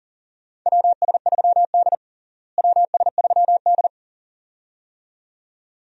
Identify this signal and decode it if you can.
Morse code, heard as a keyed tone, about 40 wpm, 700 Hz. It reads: WS3D WS3D
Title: USA Call Signs - 4 Characters - Repeated 2x - 40wpm